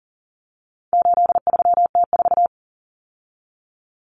Morse code, heard as a keyed tone, about 40 wpm, 700 Hz. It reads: 83T4
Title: Common Cut Numbers - 4-Digit 40wpm